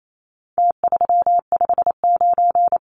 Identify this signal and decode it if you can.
Morse code, heard as a keyed tone, about 28 wpm, 700 Hz. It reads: T359
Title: Numbers - Contest Style - 4-Digit 28wpm